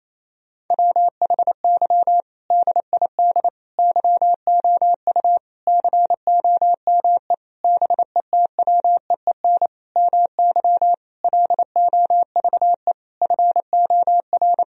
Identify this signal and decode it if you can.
Morse code, heard as a keyed tone, about 28 wpm, 700 Hz. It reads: WHY DID YOU COME BETWEEN MY LOVE FOR
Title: Sentences from Top 1000 Words - Review Part 1 - 28wpm